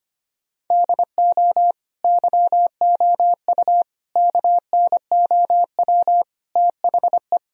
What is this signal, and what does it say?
Morse code, heard as a keyed tone, about 25 wpm, 700 Hz. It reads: DO YOU KNOW THE
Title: Sentences from Top 1000 Words - Review Part 2 - 25wpm